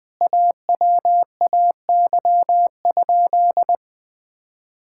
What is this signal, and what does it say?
Morse code, heard as a keyed tone, about 20 wpm, 700 Hz. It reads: AWAY?
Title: Sentences from Top 1000 Words - Review Part 6 - 20wpm